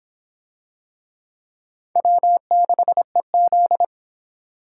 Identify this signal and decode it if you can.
Morse code, heard as a keyed tone, about 24 wpm, 700 Hz. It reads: W6EZ